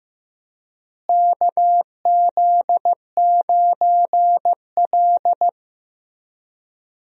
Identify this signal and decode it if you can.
Morse code, heard as a keyed tone, about 15 wpm, 700 Hz. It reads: KZ9L
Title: USA Call Signs - 4 Characters - Repeated 2x - 15wpm